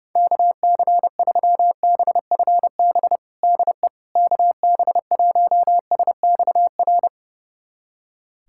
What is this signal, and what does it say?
Morse code, heard as a keyed tone, about 30 wpm, 700 Hz. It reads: KC3BFB DE KB1SXR
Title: USA Callsign DE Callsign - 6 Characters - 30wpm